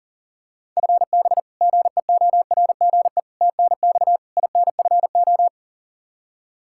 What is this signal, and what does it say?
Morse code, heard as a keyed tone, about 40 wpm, 700 Hz. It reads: FB GEORGE TNX INFO
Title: K1USN SST QSO Practice 40wpm